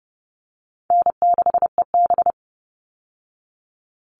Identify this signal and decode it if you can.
Morse code, heard as a keyed tone, about 30 wpm, 700 Hz. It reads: N6EB
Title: All Cut Numbers - 4-Digit 30wpm